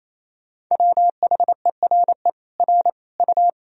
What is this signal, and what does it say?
Morse code, heard as a keyed tone, about 28 wpm, 700 Hz. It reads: WHERE R U
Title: Sentences from Top 200 Words and Abbreviations 28wpm